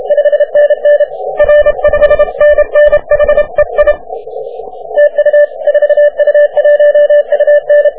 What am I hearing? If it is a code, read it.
Morse code, heard as a keyed tone, about 30 wpm, 590 Hz. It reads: 5NN R 5NN HEI TU 4U1UN
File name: wp / 4u1un 40m cw